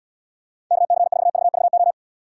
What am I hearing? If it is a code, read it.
Morse code, heard as a keyed tone, about 74 wpm, 700 Hz. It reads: KC4VVO